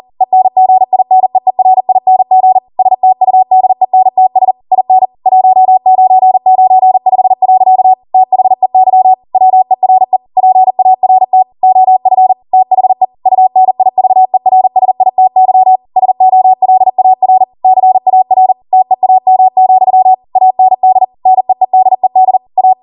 Morse code, heard as a keyed tone, about 40 wpm, 750 Hz. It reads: ENGINEERING STUDENTS IN 1995. THEY WERE PART OF THE UNIVERSITY SOLAR CAR TEAM, AND NEEDED A